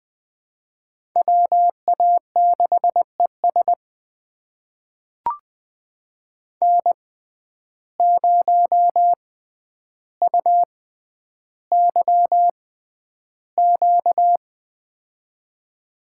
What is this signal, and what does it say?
Morse code, heard as a keyed tone, about 21 wpm, spaced as for 10 wpm, 700 Hz. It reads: WA6ES N0UYQ
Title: USA Call Signs - 5 Characters - 6x Character Spacing - 20wpm